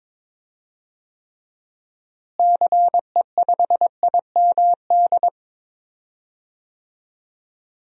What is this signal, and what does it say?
Morse code, heard as a keyed tone, about 22 wpm, 700 Hz. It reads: CE5IMD
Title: DXCC-100 Call Signs - 6 Characters - Repeated 2x - 22wpm